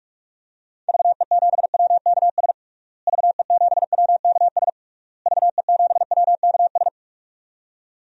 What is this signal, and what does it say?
Morse code, heard as a keyed tone, about 42 wpm, 700 Hz. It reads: VE7WKS VE7WKS VE7WKS
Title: DXCC-100 Call Signs - 6 Characters - Repeated 3x - 45wpm